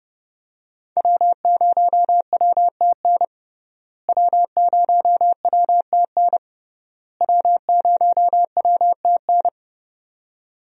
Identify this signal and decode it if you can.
Morse code, heard as a keyed tone, about 31 wpm, 700 Hz. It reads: W0WTN W0WTN W0WTN
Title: USA Call Signs - 5 Characters - Repeated 3x - 30wpm